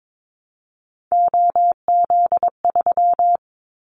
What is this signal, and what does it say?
Morse code, heard as a keyed tone, about 22 wpm, 700 Hz. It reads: OZ3
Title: Callsign Prefixes for Top 100 DXCC - 22wpm